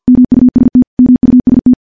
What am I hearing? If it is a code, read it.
Morse code, heard as a keyed tone, about 50 wpm, 250 Hz. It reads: MUST MUST
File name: morse / morse-training-50WPM